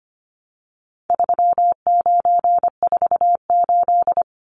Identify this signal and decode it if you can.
Morse code, heard as a keyed tone, about 25 wpm, 700 Hz. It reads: 3948
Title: Numbers - 4-Digit 25wpm